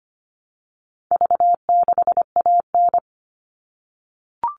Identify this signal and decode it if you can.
Morse code, heard as a keyed tone, about 25 wpm, 700 Hz. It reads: V6AN
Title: All Cut Numbers - 4-Digit 25wpm